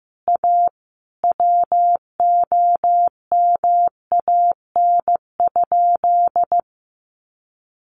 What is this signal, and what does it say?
Morse code, heard as a keyed tone, about 15 wpm, 700 Hz. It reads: A WOMAN?
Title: Sentences from Top 1100 Words 15wpm